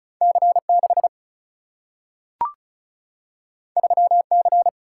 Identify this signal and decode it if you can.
Morse code, heard as a keyed tone, about 36 wpm, 700 Hz. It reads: C6 3C
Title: Single-Letter Single-Number 35wpm